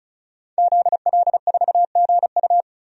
Morse code, heard as a keyed tone, about 36 wpm, 700 Hz. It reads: ZL4GU
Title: DXCC-100 Call Signs - 5 Characters - 35wpm